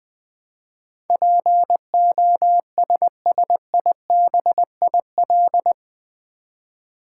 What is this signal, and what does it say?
Morse code, heard as a keyed tone, about 20 wpm, 700 Hz. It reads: POSSIBIL
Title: Build A Word - 4 Letters At A Time - 11-Letter Words - 20wpm